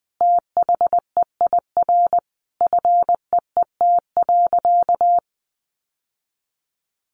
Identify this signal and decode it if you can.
Morse code, heard as a keyed tone, about 20 wpm, 700 Hz. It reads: THEIR FEET.
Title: Sentences from Top 1000 Words 20wpm